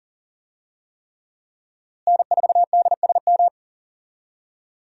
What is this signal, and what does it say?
Morse code, heard as a keyed tone, about 40 wpm, 700 Hz. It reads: N4DSM
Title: USA Call Signs - 5 Characters - Repeated 3x - 40wpm